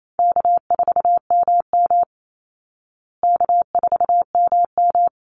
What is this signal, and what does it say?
Morse code, heard as a keyed tone, about 28 wpm, 700 Hz. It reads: K4MM K4MM
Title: Easy Call Signs Repeated 3x - 28wpm